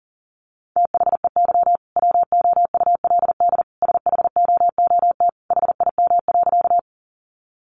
Morse code, heard as a keyed tone, about 40 wpm, 700 Hz. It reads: THEY WOULD SHOOT HIM.
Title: Sentences from Top 1200 Words 40wpm